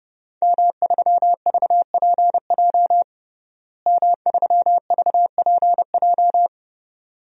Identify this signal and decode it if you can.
Morse code, heard as a keyed tone, about 31 wpm, 700 Hz. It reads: M3VPJ M3VPJ
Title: DXCC-100 Call Signs - 5 Characters - Repeated 2x - 30wpm